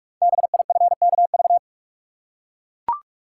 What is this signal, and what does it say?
Morse code, heard as a keyed tone, about 44 wpm, 690 Hz. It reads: BIFKV
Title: Code Groups - 5 Letter:Numbers - 45wpm